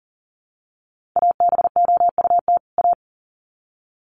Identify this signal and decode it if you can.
Morse code, heard as a keyed tone, about 40 wpm, 700 Hz. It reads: ABOUT A
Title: Sets of 2 Words 40wpm